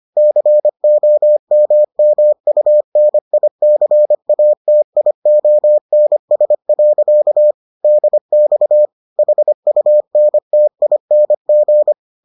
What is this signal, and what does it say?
Morse code, heard as a keyed tone, about 25 wpm, 600 Hz. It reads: COMMUNICATIONS. DX HUNTING